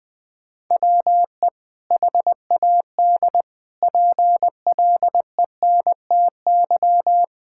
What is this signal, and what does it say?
Morse code, heard as a keyed tone, about 20 wpm, 700 Hz. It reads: WE HAD PLENTY
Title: Sentences from Top 1600 Words 20wpm